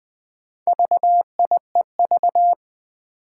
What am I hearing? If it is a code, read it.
Morse code, heard as a keyed tone, about 20 wpm, 700 Hz. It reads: VIEV
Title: Four Challenging Letter-Number Combinations 20wpm